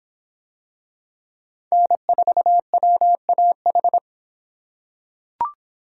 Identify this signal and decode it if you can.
Morse code, heard as a keyed tone, about 25 wpm, 700 Hz. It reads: N4WAH